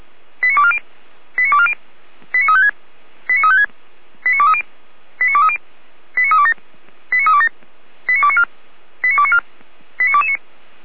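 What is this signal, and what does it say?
Five-tone selective call (ZVEI), heard as quick runs of five tones, tones of about 70 ms. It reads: 89240 89240 89356 89356 89230 89230 89327 89327 89294 89294 89229
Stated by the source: ZVEI five-tone call 89220 received on SDR, logged by feuerwehr-orlamuende fire brigade server.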